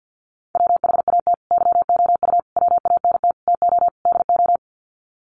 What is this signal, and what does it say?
Morse code, heard as a keyed tone, about 50 wpm, 700 Hz. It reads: WHAT YOU WANT TO DO
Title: Sets of 5 Words Encore 50wpm